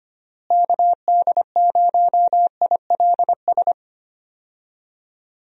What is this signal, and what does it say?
Morse code, heard as a keyed tone, about 25 wpm, 700 Hz. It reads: KD0ILS
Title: USA Call Signs - 6 Characters - Repeated 3x - 25wpm